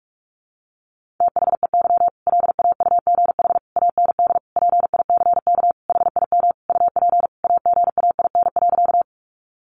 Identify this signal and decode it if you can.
Morse code, heard as a keyed tone, about 44 wpm, 700 Hz. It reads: THEY LAUGH AND PICK HIM UP AGAIN.
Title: Sentences from Top 1000 Words 45wpm